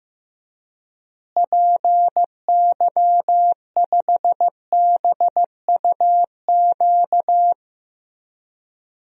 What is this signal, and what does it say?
Morse code, heard as a keyed tone, about 15 wpm, 700 Hz. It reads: PY5BUQ
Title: DXCC-100 Call Signs - 6 Characters - Repeated 3x - 15wpm